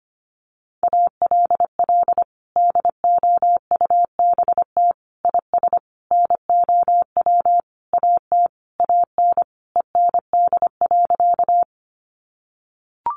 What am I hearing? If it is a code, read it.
Morse code, heard as a keyed tone, about 25 wpm, 700 Hz. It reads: ALL DOUBT IS NOW AT AN END.
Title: Sentences from Top 1500 Words 25wpm